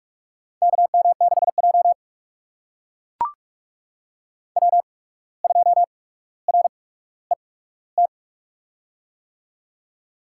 Morse code, heard as a keyed tone, about 44 wpm, spaced as for 18 wpm, 700 Hz. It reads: KM6J W2RET
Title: USA Call Signs - 8x Character Spacing - 45wpm